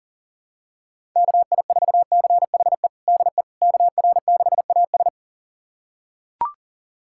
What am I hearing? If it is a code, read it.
Morse code, heard as a keyed tone, about 40 wpm, 700 Hz. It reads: KI4CHE DE KR6AS
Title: USA Callsign DE Callsign 40wpm